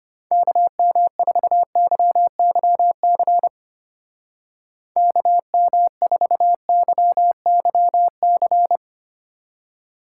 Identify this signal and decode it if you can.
Morse code, heard as a keyed tone, about 28 wpm, 700 Hz. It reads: KM4YYC KM4YYC
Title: Speed Racing - USA Call Signs 25wpm